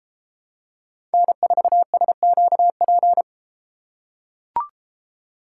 Morse code, heard as a keyed tone, about 33 wpm, 700 Hz. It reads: N4SQP